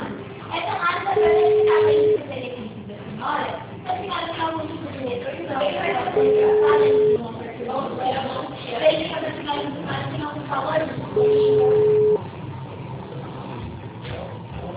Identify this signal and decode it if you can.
Ringback tone (Europe, 425 Hz), with speech.